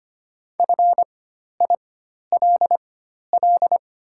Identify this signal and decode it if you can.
Morse code, heard as a keyed tone, about 25 wpm, spaced as for 12 wpm, 700 Hz. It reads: FILL
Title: 4x Character Spacing - Top 500 Words - 25wpm